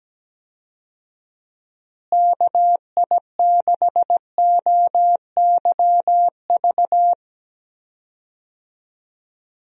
Morse code, heard as a keyed tone, about 17 wpm, 700 Hz. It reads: KI6OYV
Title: USA Call Signs - 6 Characters - Repeated 2x - 17wpm